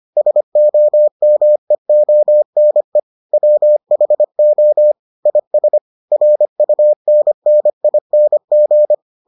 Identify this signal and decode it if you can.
Morse code, heard as a keyed tone, about 25 wpm, 600 Hz. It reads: SOMEONE WHO IS RUNNING